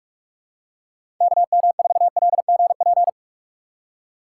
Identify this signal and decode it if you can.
Morse code, heard as a keyed tone, about 46 wpm, 700 Hz. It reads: KM4LGP
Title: USA Call Signs - 4x Character Spacing - 45wpm